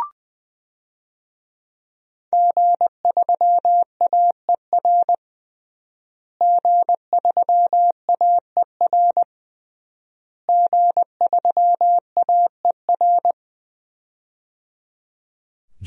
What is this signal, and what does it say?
Morse code, heard as a keyed tone, about 21 wpm, 700 Hz. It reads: G3AER G3AER G3AER
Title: DXCC-100 Call Signs - 5 Characters - Repeated 3x - 20wpm